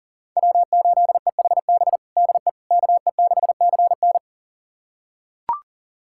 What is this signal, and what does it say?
Morse code, heard as a keyed tone, about 40 wpm, 700 Hz. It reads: W8EHB DE KE6CN
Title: USA Callsign DE Callsign - 5 Characters - 40wpm